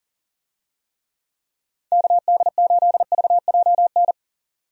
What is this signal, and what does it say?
Morse code, heard as a keyed tone, about 40 wpm, 700 Hz. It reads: KD8VJN
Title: USA Call Signs - 6 Characters - 40wpm